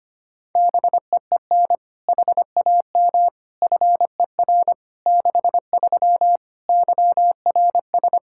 Morse code, heard as a keyed tone, about 25 wpm, 700 Hz. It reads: BEEN HAM FER 63 YRS
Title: Practice QSO - Exchange 3 - More Info - 25wpm